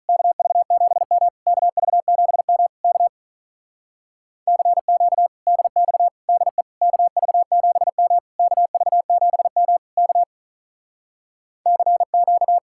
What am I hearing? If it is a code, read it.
Morse code, heard as a keyed tone, about 45 wpm, 700 Hz. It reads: KV7M KV7M K CQ DX DE KV7M KV7M K CQ